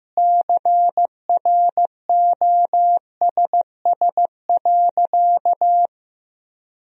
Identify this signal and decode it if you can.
Morse code, heard as a keyed tone, about 15 wpm, 700 Hz. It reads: CROSS.
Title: Sentences from Top 1000 Words - Review Part 2 - 15wpm